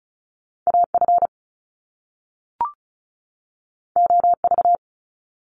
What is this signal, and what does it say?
Morse code, heard as a keyed tone, about 36 wpm, 700 Hz. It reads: AF OV
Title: Top Two Letter Combinations in Words 35wpm